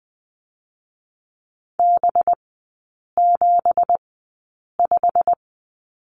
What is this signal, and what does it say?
Morse code, heard as a keyed tone, about 20 wpm, spaced as for 8 wpm, 700 Hz. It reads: B75
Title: 184 Lesson 30e - Letter b - Three Characters 20wpm